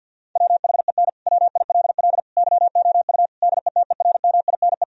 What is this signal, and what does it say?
Morse code, heard as a keyed tone, about 50 wpm, 700 Hz. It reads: WHEN WILL YOU DETERMINE
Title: Sentences from Top 1000 Words - Review Part 4 - 50wpm